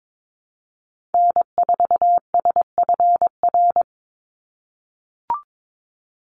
Morse code, heard as a keyed tone, about 22 wpm, 700 Hz. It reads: N4SFR
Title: NA SOTA Activator Callsigns 22wpm